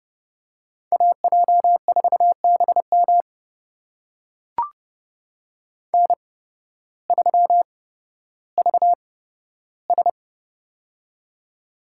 Morse code, heard as a keyed tone, about 30 wpm, spaced as for 14 wpm, 700 Hz. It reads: AJ4BM N3VS
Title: USA Call Signs - 8x Character Spacing - 30wpm